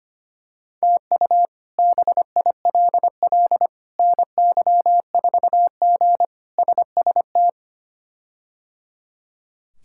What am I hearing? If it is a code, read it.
Morse code, heard as a keyed tone, about 25 wpm, 700 Hz. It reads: TU BILL NY4G SST
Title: K1USN SST QSO Practice - 25wpm